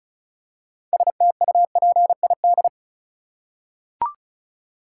Morse code, heard as a keyed tone, about 36 wpm, 700 Hz. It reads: STUPID
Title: Six Letter Words - 35wpm